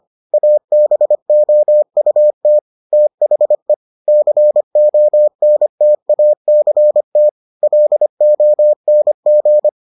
Morse code, heard as a keyed tone, about 25 wpm, 600 Hz. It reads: ABOUT THE CONTACT LONG